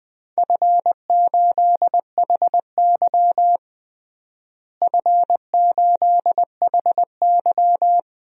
Morse code, heard as a keyed tone, about 20 wpm, 700 Hz. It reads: F8HY F8HY
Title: DXCC-100 Call Signs - 4 Characters - Repeated 3x - 20wpm